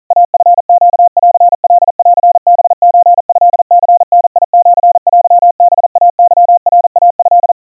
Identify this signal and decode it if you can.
Morse code (Japanese Wabun code), heard as a keyed tone, about 40 wpm, 700 Hz. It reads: イチネンカンハソトニダシテハイケナイト
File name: kawa / Morse sample / kamyu 150 cpm 700 Hz Crispy